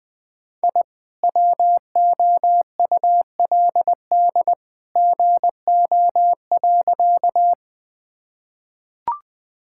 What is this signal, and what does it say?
Morse code, heard as a keyed tone, about 20 wpm, 700 Hz. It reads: I WOULD GO.
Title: Sentences with 2-Word Phrases - 20wpm